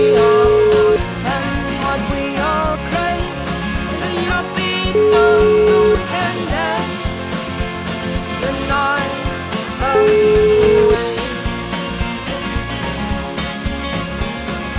Ringback tone (Europe, 425 Hz), with noise.